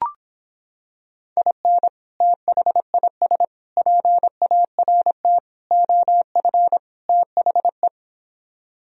Morse code, heard as a keyed tone, about 26 wpm, 700 Hz. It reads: IN THIS PART OF THE